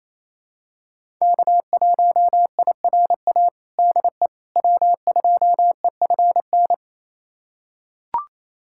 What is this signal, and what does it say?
Morse code, heard as a keyed tone, about 27 wpm, 700 Hz. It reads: K1IRA DE W2EFN
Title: USA Callsign DE Callsign - 5 Characters - 28wpm